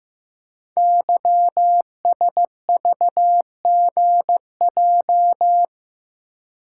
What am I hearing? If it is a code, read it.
Morse code, heard as a keyed tone, about 15 wpm, 700 Hz. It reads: YSVGJ
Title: Code Groups - 5 Letters - 15wpm